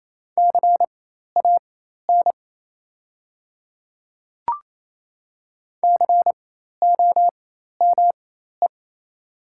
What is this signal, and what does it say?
Morse code, heard as a keyed tone, about 28 wpm, spaced as for 13 wpm, 700 Hz. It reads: CAN COME
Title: 4x Character Spacing - Top 100 Words - 28wpm